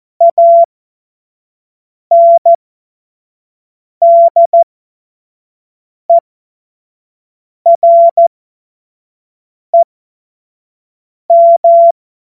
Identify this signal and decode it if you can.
Morse code, heard as a keyed tone, about 14 wpm, spaced as for 5 wpm, 680 Hz. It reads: ANDEREM
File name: Morsekurs / Witze / Witz 27. Juni 2023 25 F70